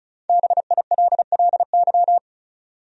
Morse code, heard as a keyed tone, about 34 wpm, 700 Hz. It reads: BILLY
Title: USA Common Names - 35wpm